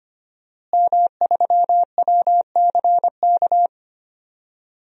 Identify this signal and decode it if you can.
Morse code, heard as a keyed tone, about 25 wpm, 700 Hz. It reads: M3WCK
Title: DXCC-100 Call Signs - 5 Characters - Repeated 2x - 25wpm